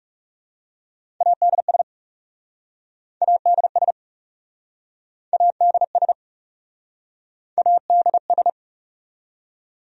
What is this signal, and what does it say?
Morse code, heard as a keyed tone, about 40 wpm, 700 Hz. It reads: ADS ADS ADS ADS